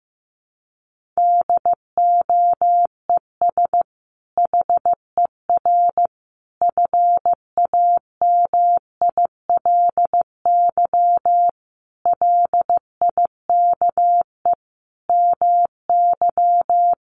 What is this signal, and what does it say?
Morse code, heard as a keyed tone, about 15 wpm, 700 Hz. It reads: DOES HER FAMILY LIKE MY
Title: Sentences from Top 1400 Words 15wpm